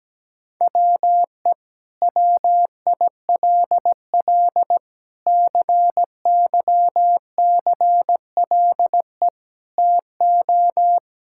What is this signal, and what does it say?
Morse code, heard as a keyed tone, about 17 wpm, 700 Hz. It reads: WE WILL CYCLE TO
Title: Sentences from Top 1000 Words - Review Part 6 - 17wpm